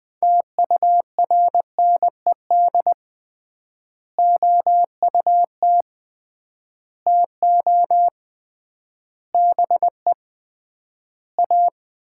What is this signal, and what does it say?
Morse code, heard as a keyed tone, about 21 wpm, 700 Hz. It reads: TURNED OUT TO BE A
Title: 3x Word-Spacing - Sets of 5 Words - 20wpm